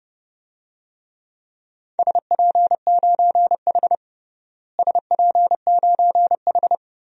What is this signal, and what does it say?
Morse code, heard as a keyed tone, about 30 wpm, 700 Hz. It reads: SP9H SP9H
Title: DXCC-100 Call Signs - 4 Characters - Repeated 3x - 30wpm